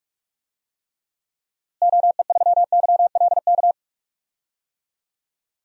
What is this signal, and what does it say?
Morse code, heard as a keyed tone, about 46 wpm, 700 Hz. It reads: OE3YLK
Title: DXCC-100 Call Signs - 6 Characters - Repeated 3x - 45wpm